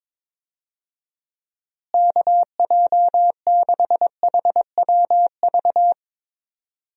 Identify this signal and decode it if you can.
Morse code, heard as a keyed tone, about 22 wpm, 700 Hz. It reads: KJ6HWV
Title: USA Call Signs - 6 Characters - Repeated 2x - 22wpm